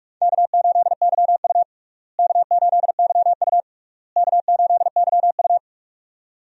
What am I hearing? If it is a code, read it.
Morse code, heard as a keyed tone, about 46 wpm, 700 Hz. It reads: K8YU K8YU K8YU
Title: USA Call Signs - 4 Characters - Repeated 3x - 45wpm